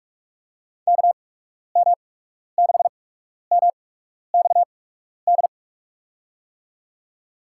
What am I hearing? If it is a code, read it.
Morse code, heard as a keyed tone, about 47 wpm, spaced as for 12 wpm, 700 Hz. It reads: KM6MXD
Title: USA Call Signs - 6 Characters - 8x Character Spacing - 45wpm